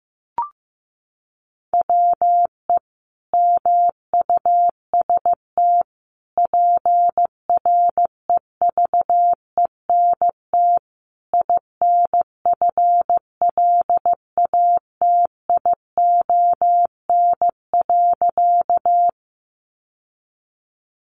Morse code, heard as a keyed tone, about 15 wpm, 700 Hz. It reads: WE MUST PREVENT INFLATION.
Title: Sentences from Top 1100 Words 15wpm